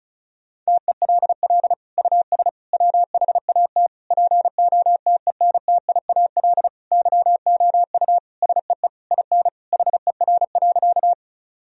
Morse code, heard as a keyed tone, about 36 wpm, 700 Hz. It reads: TELL US WHAT POTENTIAL YOU SEE IN HER.
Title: Sentences from Top 1000 Words - Review Part 4 - 35wpm